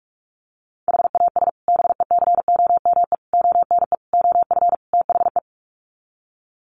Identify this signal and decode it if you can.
Morse code, heard as a keyed tone, about 44 wpm, 700 Hz. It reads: HAS BECOME ONE OF THE
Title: Sets of 5 Words Encore 45wpm